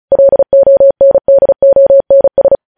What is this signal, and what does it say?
Morse code, heard as a keyed tone, about 36 wpm, 550 Hz. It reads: LONDONS
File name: podcast / News headlines-1x-35x35